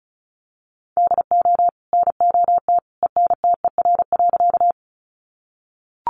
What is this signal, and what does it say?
Morse code, heard as a keyed tone, about 34 wpm, 700 Hz. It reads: DO NOT ENTER.
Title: Sentences from Top 1300 Words 35wpm